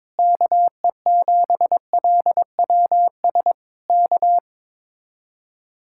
Morse code, heard as a keyed tone, about 21 wpm, 700 Hz. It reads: KE7LWS K
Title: Speed-Racing Common QSO Phrases Encore - 22wpm